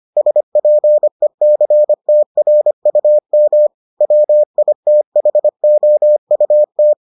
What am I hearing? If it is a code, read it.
Morse code, heard as a keyed tone, about 25 wpm, 600 Hz. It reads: SPECTRUM WITHOUT